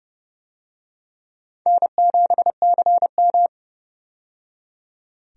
Morse code, heard as a keyed tone, about 32 wpm, 700 Hz. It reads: N7CM
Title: USA Call Signs - 4 Characters - 30wpm